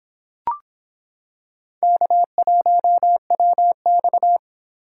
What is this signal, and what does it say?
Morse code, heard as a keyed tone, about 26 wpm, 700 Hz. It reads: K1WX